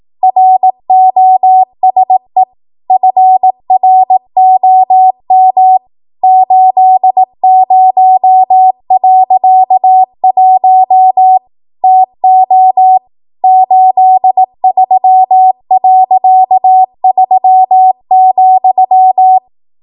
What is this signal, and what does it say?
Morse code, heard as a keyed tone, about 18 wpm, 750 Hz. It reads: ROSE FROM 80.1 TO 83.3,
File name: Archive / 18 WPM / 18WPM